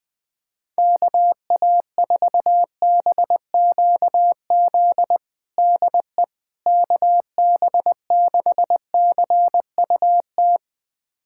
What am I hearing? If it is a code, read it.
Morse code, heard as a keyed tone, about 20 wpm, 700 Hz. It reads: KA4BQZ DE KB6CUT
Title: USA Callsign DE Callsign - 6 Characters - 20wpm